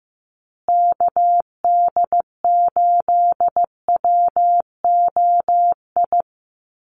Morse code, heard as a keyed tone, about 15 wpm, 700 Hz. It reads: KD8WOI
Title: USA Call Signs Repeated 2x 15wpm